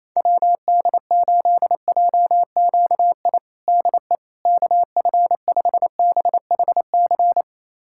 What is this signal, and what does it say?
Morse code, heard as a keyed tone, about 28 wpm, 700 Hz. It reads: WD8JQI DE KF5BHC
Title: USA Callsign DE Callsign 28wpm